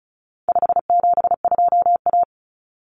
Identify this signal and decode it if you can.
Morse code, heard as a keyed tone, about 34 wpm, 700 Hz. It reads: 572A
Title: Common Cut Numbers - 4-Digit 35wpm